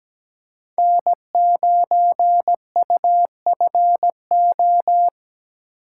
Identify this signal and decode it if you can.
Morse code, heard as a keyed tone, about 20 wpm, 700 Hz. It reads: N9UFO
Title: Speed Racing - USA Call Signs 17wpm